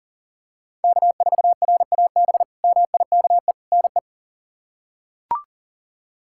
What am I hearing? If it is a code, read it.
Morse code, heard as a keyed tone, about 37 wpm, 700 Hz. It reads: K4RAB MIKE NE
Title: K1USN SST QSO Practice - 40wpm